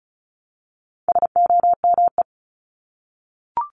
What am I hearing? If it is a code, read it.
Morse code, heard as a keyed tone, about 34 wpm, 700 Hz. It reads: SOME
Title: Top 500 Words Repeated 35wpm